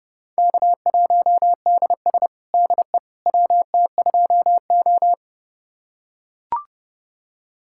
Morse code, heard as a keyed tone, about 29 wpm, 700 Hz. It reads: K1DS DE WT2O
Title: USA Callsign DE Callsign - 4 Characters - 30wpm